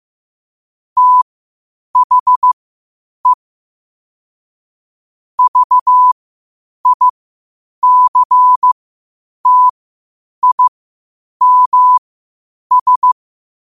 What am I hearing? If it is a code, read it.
Morse code, heard as a keyed tone, about 15 wpm, spaced as for 8 wpm, 1000 Hz. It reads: THE VICTIMS
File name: qotdfiles / 8wpm